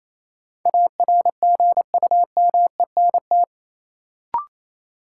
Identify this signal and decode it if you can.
Morse code, heard as a keyed tone, about 28 wpm, 700 Hz. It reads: ARGUMENT
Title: Sentences from Top 1400 Words - New Word Drill - 28wpm